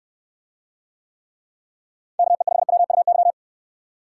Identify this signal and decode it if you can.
Morse code, heard as a keyed tone, about 65 wpm, 700 Hz. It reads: KE5WUY